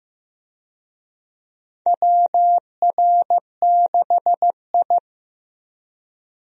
Morse code, heard as a keyed tone, about 16 wpm, 700 Hz. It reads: WR6I
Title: USA Call Signs - 4 Characters - Repeated 2x - 15wpm